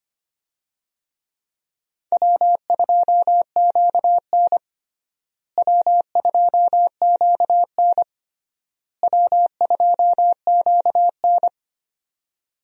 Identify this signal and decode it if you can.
Morse code, heard as a keyed tone, about 25 wpm, 700 Hz. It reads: W2QN W2QN W2QN
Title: USA Call Signs - 4 Characters - Repeated 3x - 25wpm